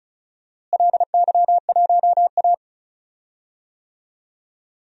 Morse code, heard as a keyed tone, about 34 wpm, 690 Hz. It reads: LY1A
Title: DXCC-100 Call Signs - 4 Characters - Repeated 2x - 35wpm